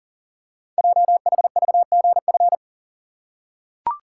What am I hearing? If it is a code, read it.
Morse code, heard as a keyed tone, about 39 wpm, 700 Hz. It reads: JHVGF
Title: Code Groups - 5 Letters - 40wpm